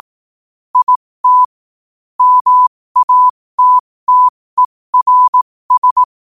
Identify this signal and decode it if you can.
Morse code, heard as a keyed tone, about 18 wpm, spaced as for 15 wpm, 1000 Hz. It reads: IT MATTERS
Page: 15 WPM morse code quotes for Tue, 23 Sep 2025.